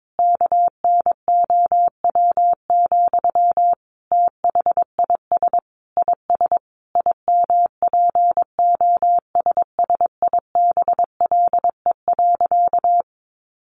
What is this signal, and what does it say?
Morse code, heard as a keyed tone, about 22 wpm, 700 Hz. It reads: KNOW, THIS IS IMPOSSIBLE.
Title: Sentences from Top 1100 Words 22wpm